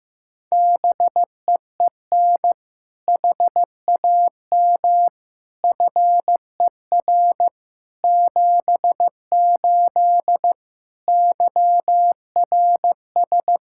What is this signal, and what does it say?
Morse code, heard as a keyed tone, about 15 wpm, 700 Hz. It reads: BEEN HAM FER 78 YRS
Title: Practice QSO - Exchange 3 - More Info - 15wpm